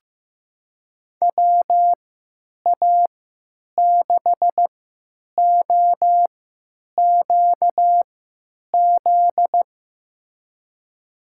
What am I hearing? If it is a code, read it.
Morse code, heard as a keyed tone, about 15 wpm, spaced as for 9 wpm, 700 Hz. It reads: WA6OQZ
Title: USA Call Signs - 6 Characters - 3x Character Spacing - 15wpm